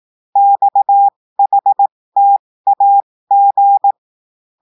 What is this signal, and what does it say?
Morse code, heard as a keyed tone, about 18 wpm, spaced as for 15 wpm, 800 Hz. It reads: XHTAG
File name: cours-du-lundi files / cours-1580-texte-chiffre-15-mpm